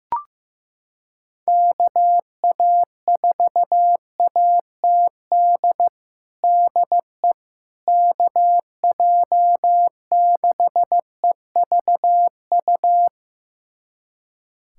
Morse code, heard as a keyed tone, about 16 wpm, 700 Hz. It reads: KA4ATD DE KJ6EVU
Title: USA Callsign DE Callsign - 6 Characters - 15wpm